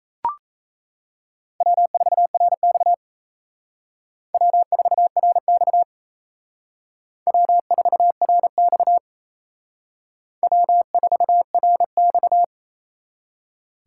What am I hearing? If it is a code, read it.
Morse code, heard as a keyed tone, about 36 wpm, 700 Hz. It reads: W4RX W4RX W4RX W4RX